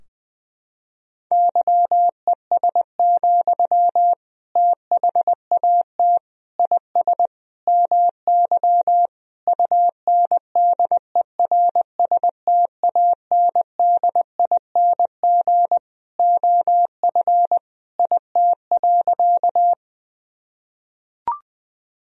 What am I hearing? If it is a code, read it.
Morse code, heard as a keyed tone, about 20 wpm, 700 Hz. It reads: YES, THAT IS MY UNDERSTANDING OF IT.
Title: Sentences from Top 1000 Words - Review Part 3 - 20wpm